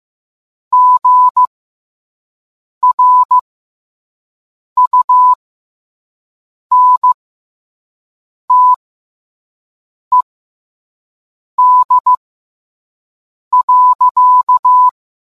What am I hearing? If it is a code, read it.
Morse code, heard as a keyed tone, about 15 wpm, spaced as for 5 wpm, 1000 Hz. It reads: GRUNTED.